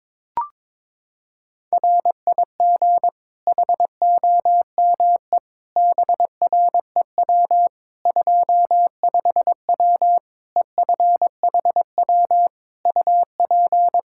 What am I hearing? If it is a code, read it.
Morse code, heard as a keyed tone, about 22 wpm, 700 Hz. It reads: RIG HOME BREW 25W EFHW UP
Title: K1USN SST QSO Practice 22wpm